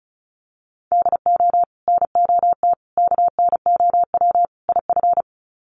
Morse code, heard as a keyed tone, about 34 wpm, 700 Hz. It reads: DO NOT KNOW IF
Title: Sets of 4 Words 35wpm